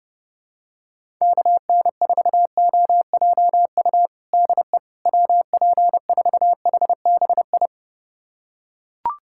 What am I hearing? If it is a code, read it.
Morse code, heard as a keyed tone, about 30 wpm, 700 Hz. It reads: KN4OJU DE WP4HBI
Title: USA Callsign DE Callsign 30wpm